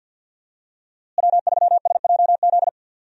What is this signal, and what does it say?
Morse code, heard as a keyed tone, about 48 wpm, 700 Hz. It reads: W3SJZ
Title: USA Call Signs - 5 Characters - Repeated 2x - 50wpm